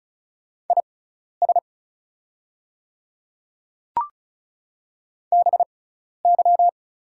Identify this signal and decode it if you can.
Morse code, heard as a keyed tone, about 34 wpm, spaced as for 12 wpm, 700 Hz. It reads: IS BY
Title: 6x Character Spacing - Top 100 Words - 35wpm